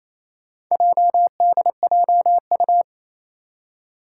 Morse code, heard as a keyed tone, about 29 wpm, 700 Hz. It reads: JDJU
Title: Four Challenging Letter-Number Combinations 28wpm